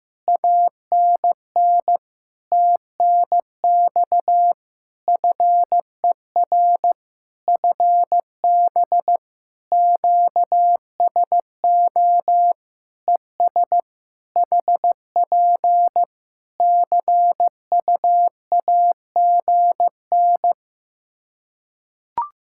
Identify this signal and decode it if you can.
Morse code, heard as a keyed tone, about 15 wpm, 700 Hz. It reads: ANN TNX FER FB QSO ES HP CUAGN
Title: K1USN SST QSO Practice 15wpm